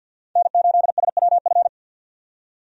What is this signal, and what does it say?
Morse code, heard as a keyed tone, about 52 wpm, 700 Hz. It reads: N8SWF
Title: USA Call Signs - 5 Characters - 50wpm